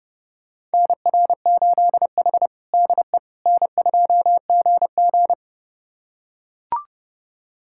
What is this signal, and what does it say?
Morse code, heard as a keyed tone, about 32 wpm, 700 Hz. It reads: NR8H DE N2GG
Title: USA Callsign DE Callsign - 4 Characters - 30wpm